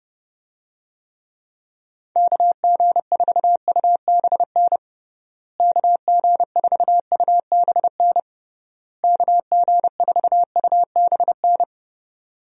Morse code, heard as a keyed tone, about 31 wpm, 700 Hz. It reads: KG4UBN KG4UBN KG4UBN
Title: USA Call Signs - 6 Characters - Repeated 3x - 30wpm